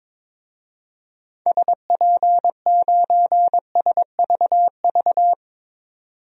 Morse code, heard as a keyed tone, about 22 wpm, 700 Hz. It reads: SP9SVV
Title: DXCC-100 Call Signs - 6 Characters - 22wpm